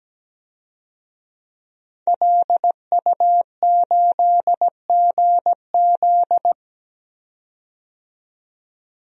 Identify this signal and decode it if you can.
Morse code, heard as a keyed tone, about 17 wpm, 700 Hz. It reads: LU8GZ
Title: DXCC-100 Call Signs - 5 Characters - 17wpm